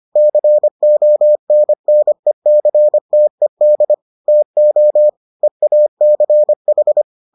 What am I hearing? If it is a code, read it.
Morse code, heard as a keyed tone, about 25 wpm, 600 Hz. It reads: CONNECTED TO EACH